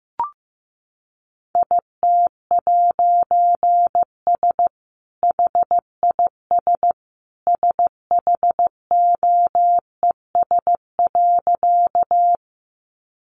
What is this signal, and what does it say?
Morse code, heard as a keyed tone, about 15 wpm, 700 Hz. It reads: IT'S HIS SHOES.
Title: A Sound Of Thunder 15wpm